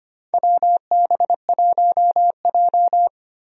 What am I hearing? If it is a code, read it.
Morse code, heard as a keyed tone, about 25 wpm, 700 Hz. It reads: WB1J
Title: USA Call Signs - 4 Characters - 25wpm